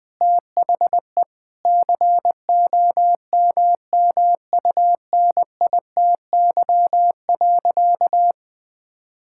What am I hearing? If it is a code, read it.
Morse code, heard as a keyed tone, about 20 wpm, 700 Hz. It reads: THE COMMUNITY.
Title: Sentences from Top 1000 Words - Review Part 5 - 20wpm